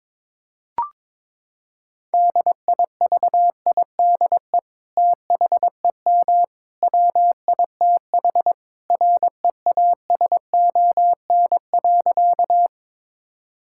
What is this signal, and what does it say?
Morse code, heard as a keyed tone, about 22 wpm, 700 Hz. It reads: DIVIDE THEM WITH REASON.
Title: Sentences from Top 1600 Words 22wpm